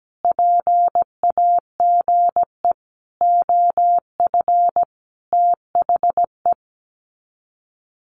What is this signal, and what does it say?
Morse code, heard as a keyed tone, about 17 wpm, 700 Hz. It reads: PAGE OF THE
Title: Sets of 5 Words Encore 17wpm